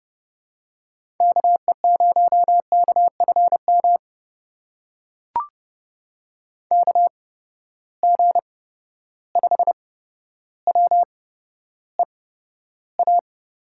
Morse code, heard as a keyed tone, about 29 wpm, spaced as for 13 wpm, 700 Hz. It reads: KE0KFM KG5WEA
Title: USA Call Signs - 6 Characters - 8x Character Spacing - 30wpm